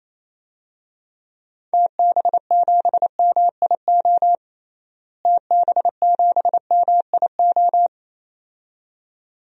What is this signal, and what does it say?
Morse code, heard as a keyed tone, about 28 wpm, 700 Hz. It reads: TB7MIO TB7MIO
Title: DXCC-100 Call Signs - 6 Characters - Repeated 2x - 28wpm